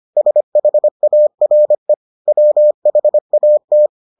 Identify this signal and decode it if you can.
Morse code, heard as a keyed tone, about 25 wpm, 600 Hz. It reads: SHARE WHAT